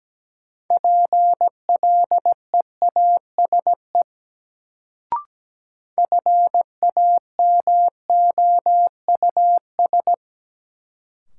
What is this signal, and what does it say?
Morse code, heard as a keyed tone, about 17 wpm, 700 Hz. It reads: PLEASE FAMOUS
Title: Sentences from Top 1500 Words - New Word Drill Alt - 17wpm